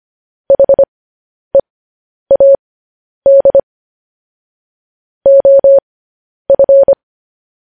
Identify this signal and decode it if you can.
Morse code, heard as a keyed tone, about 25 wpm, spaced as for 10 wpm, 550 Hz. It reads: HEAD OF
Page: News Headlines in Morse Code at 10 WPM